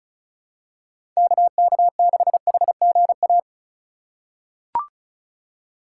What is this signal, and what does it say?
Morse code, heard as a keyed tone, about 36 wpm, 700 Hz. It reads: KK6HGA
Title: USA Call Signs - 6 Characters - 3x Character Spacing - 35wpm